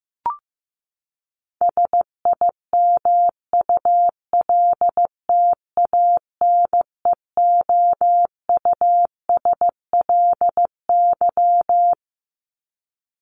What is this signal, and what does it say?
Morse code, heard as a keyed tone, about 15 wpm, 700 Hz. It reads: SIMULTANEOUSLY
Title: LY Suffix - Top 100 Words with Base Words 15wpm